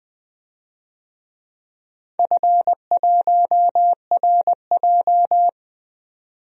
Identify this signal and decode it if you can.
Morse code, heard as a keyed tone, about 20 wpm, 700 Hz. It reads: F1RJ
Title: DXCC-100 Call Signs - 4 Characters - 20wpm